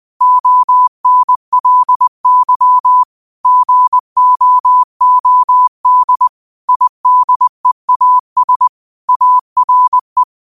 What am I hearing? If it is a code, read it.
Morse code, heard as a keyed tone, about 20 wpm, 1000 Hz. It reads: ONLY GOOD IDEAS ARE
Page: 20 WPM morse code quotes for Tue, 12 Nov 2024.